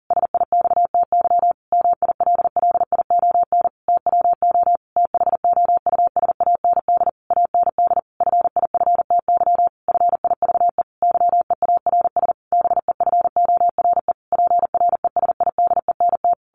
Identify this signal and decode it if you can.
Morse code, heard as a keyed tone, about 40 wpm, 700 Hz. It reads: SIXTY MILLION TWO THOUSAND AND FIFTY FIVE YEARS BEFORE PRESIDENT
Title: A Sound Of Thunder 40wpm